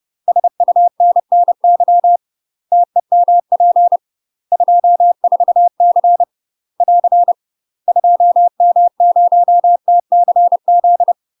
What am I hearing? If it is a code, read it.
Morse code, heard as a keyed tone, about 30 wpm, 700 Hz. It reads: SUNNY TEMP 24C <AR> 2M0TCZ
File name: UK MASTER QSOs TRAINING/30WPM